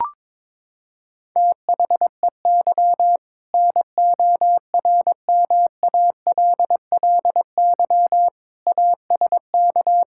Morse code, heard as a keyed tone, about 22 wpm, 700 Hz. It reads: THEY NORMALLY ASK
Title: Sentences from Top 1000 Words - Review Part 6 - 22wpm